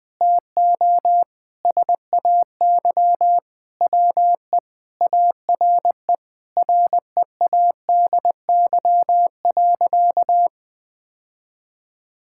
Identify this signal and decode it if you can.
Morse code, heard as a keyed tone, about 20 wpm, 700 Hz. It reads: TO SAY WE ARE READY.
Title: Sentences with 6-Word Phrases - Spoken Beginning - 20wpm